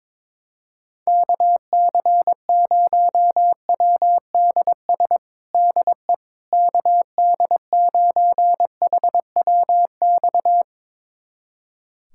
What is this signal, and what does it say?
Morse code, heard as a keyed tone, about 22 wpm, 700 Hz. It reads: KC0WDS DE KD9HWX
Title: USA Callsign DE Callsign - 6 Characters - 22wpm